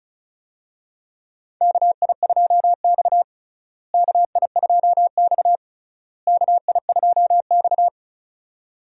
Morse code, heard as a keyed tone, about 35 wpm, 700 Hz. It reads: KI2X KI2X KI2X
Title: USA Call Signs - 4 Characters - Repeated 3x - 35wpm